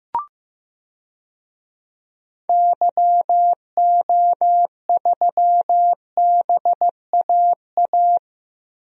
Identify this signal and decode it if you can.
Morse code, heard as a keyed tone, about 16 wpm, 700 Hz. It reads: YO3BAA
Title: DXCC-100 Call Signs - 6 Characters - Repeated 3x - 15wpm